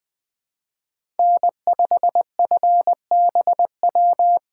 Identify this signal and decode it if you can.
Morse code, heard as a keyed tone, about 20 wpm, 700 Hz. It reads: N5FBW
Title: Speed Racing - USA 1x3 Call Signs 17wpm